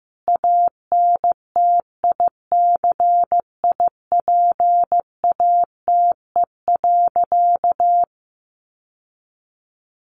Morse code, heard as a keyed tone, about 15 wpm, 700 Hz. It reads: ANTICIPATE.
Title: Sentences from Top 1500 Words 15wpm